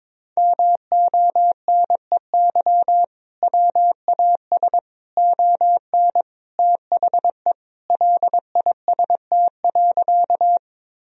Morse code, heard as a keyed tone, about 22 wpm, 700 Hz. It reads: MONEY WAS ON THE LIST.
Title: Sentences from Top 1000 Words - Review Part 2 - 22wpm